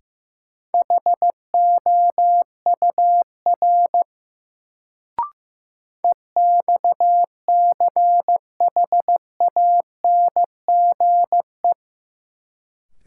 Morse code, heard as a keyed tone, about 15 wpm, 700 Hz. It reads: HOUR EXCHANGE
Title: Sets of 5 Words - New Word Drill - 15wpm